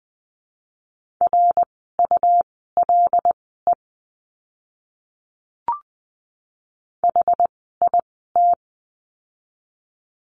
Morse code, heard as a keyed tone, about 20 wpm, spaced as for 14 wpm, 700 Hz. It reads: RULE HIT
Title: 101 Lesson 17f - Letter u - Words 20wpm